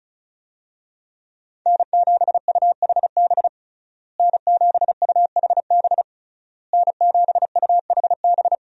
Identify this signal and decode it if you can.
Morse code, heard as a keyed tone, about 34 wpm, 700 Hz. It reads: N7UHB N7UHB N7UHB
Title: USA Call Signs - 5 Characters - Repeated 3x - 35wpm